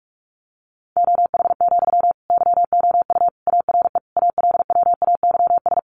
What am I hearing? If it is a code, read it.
Morse code, heard as a keyed tone, about 44 wpm, 700 Hz. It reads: OH, YOU ARE ALWAYS
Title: Sentences from Top 900 Words 45wpm